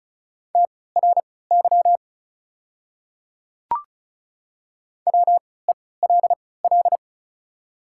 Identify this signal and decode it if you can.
Morse code, heard as a keyed tone, about 36 wpm, spaced as for 20 wpm, 700 Hz. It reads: TRY WELL
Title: 3x Character Spacing - Top 200 Words - 35wpm